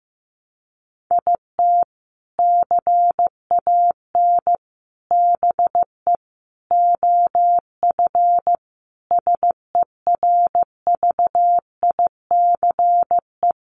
Morse code, heard as a keyed tone, about 15 wpm, 700 Hz. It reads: IT CAN BE OF SERVICE
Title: Sentences from Top 1200 Words 15wpm